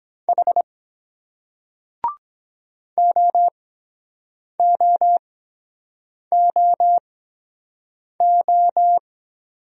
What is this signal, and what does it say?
Morse code, heard as a keyed tone, about 23 wpm, spaced as for 7 wpm, 700 Hz. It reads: H OOOO